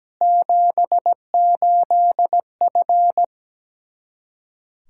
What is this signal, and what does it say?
Morse code, heard as a keyed tone, about 17 wpm, 700 Hz. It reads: 78F
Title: K1USN SST QSO Practice 17wpm